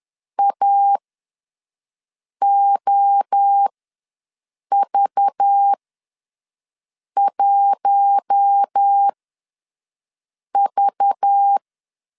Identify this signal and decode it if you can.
Morse code, heard as a keyed tone, about 10 wpm, spaced as for 5 wpm, 800 Hz. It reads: AOV1V
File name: LEZIONI / 5 LBVC / 5 LBV ws 10wpm 3645